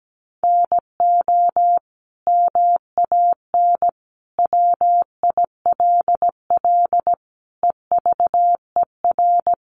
Morse code, heard as a keyed tone, about 17 wpm, 700 Hz. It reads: NO MAN WILL EVER
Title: Sentences from Top 1300 Words 17wpm